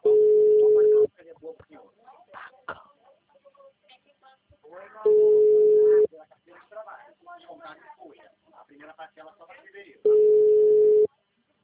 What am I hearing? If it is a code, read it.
Ringback tone (Europe, 425 Hz), with speech.